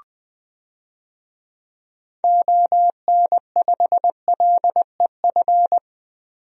Morse code, heard as a keyed tone, about 20 wpm, 700 Hz. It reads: ON5LEF
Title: DXCC-100 Call Signs - 6 Characters - 20wpm